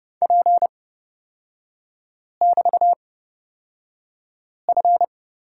Morse code, heard as a keyed tone, about 30 wpm, spaced as for 5 wpm, 700 Hz. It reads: P<BT>F
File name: Lesson 36e - BT - Three Characters 30wpm